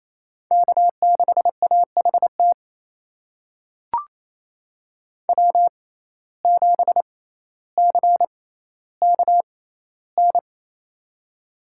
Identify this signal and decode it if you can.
Morse code, heard as a keyed tone, about 28 wpm, spaced as for 14 wpm, 700 Hz. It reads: K6AHT W7CKN
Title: USA Call Signs - 5 Characters - 6x Character Spacing - 28wpm